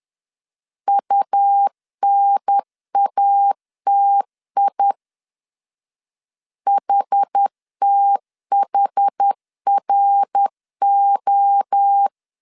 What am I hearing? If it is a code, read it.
Morse code, heard as a keyed tone, about 10 wpm, 800 Hz. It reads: UNATI HTHRO
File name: CW / DET / 1 3 1500C S B 10wpm 3335